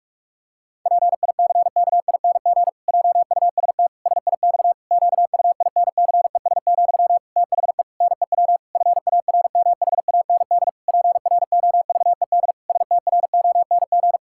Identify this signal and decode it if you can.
Morse code, heard as a keyed tone, about 46 wpm, 700 Hz. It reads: PICKING JUST SIX QUINCES, THE NEW FARMHAND PROVED STRONG
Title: Pangram Challenge v2 - 45wpm